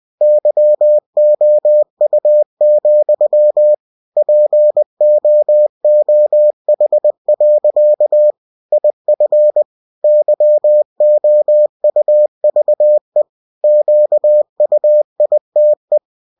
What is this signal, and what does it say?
Morse code, heard as a keyed tone, about 20 wpm, 600 Hz. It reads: YOU, POOH. IF YOUVE QUITE
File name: cw-books / winnie-the-pooh / winnie-20wpm-page-0084